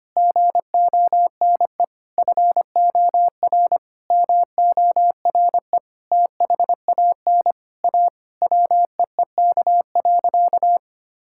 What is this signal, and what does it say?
Morse code, heard as a keyed tone, about 25 wpm, 700 Hz. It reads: GONE FOR MORE THAN A WEEK.
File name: practice / 25wpm / Sentences with 4-Word Phrases - 25wpm